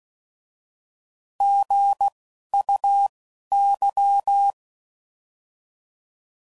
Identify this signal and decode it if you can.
Morse code, heard as a keyed tone, about 16 wpm, spaced as for 11 wpm, 800 Hz. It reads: GUY
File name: Prefixes / By First Letter